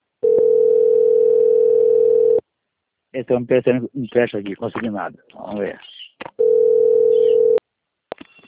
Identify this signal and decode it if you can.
Ringback tone (North America), 440 + 480 Hz, with speech.
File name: out- - - - - .38235